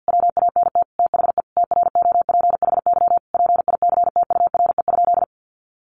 Morse code, heard as a keyed tone, about 50 wpm, 700 Hz. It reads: WANT THE TROPHY PICTURE?
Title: A Sound Of Thunder 50wpm